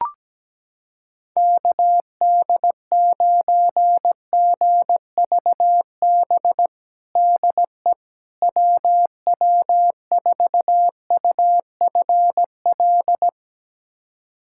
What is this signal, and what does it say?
Morse code, heard as a keyed tone, about 17 wpm, 700 Hz. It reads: KD9GVB DE WW4UFL
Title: USA Callsign DE Callsign - 6 Characters - 17wpm